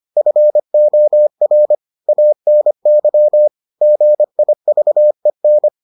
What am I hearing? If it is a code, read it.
Morse code, heard as a keyed tone, about 25 wpm, 600 Hz. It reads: FOR ANY GIVEN